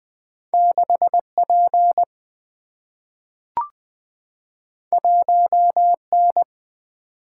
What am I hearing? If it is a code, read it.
Morse code, heard as a keyed tone, about 20 wpm, 700 Hz. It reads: 6 P 1 N
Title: Single-Letter Single-Number 20wpm